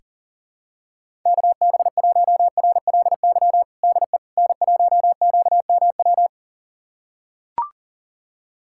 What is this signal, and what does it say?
Morse code, heard as a keyed tone, about 40 wpm, 700 Hz. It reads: KB1RLY DE N1QMW
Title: USA Callsign DE Callsign 40wpm